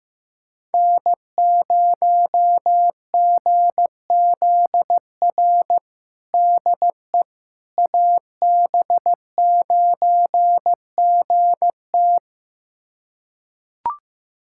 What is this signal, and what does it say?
Morse code, heard as a keyed tone, about 16 wpm, 700 Hz. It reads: N0GZR DE AB9GT
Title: USA Callsign DE Callsign - 5 Characters - 15wpm